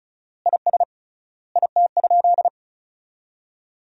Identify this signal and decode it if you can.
Morse code, heard as a keyed tone, about 34 wpm, 700 Hz. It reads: IS IT?
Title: 3x Word-Spacing - Sentences from Top 100 Words - 35wpm